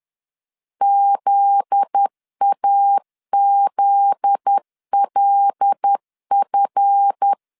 Morse code, heard as a keyed tone, about 10 wpm, 800 Hz. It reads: ZAZLF
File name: LEZIONI / 7 GF9Z / 7 GF9Z 10wpm 2611